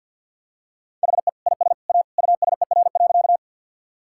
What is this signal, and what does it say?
Morse code, heard as a keyed tone, about 53 wpm, 700 Hz. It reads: HE IS A USER.
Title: Sentences from Top 1000 Words - Review Part 5 - 50wpm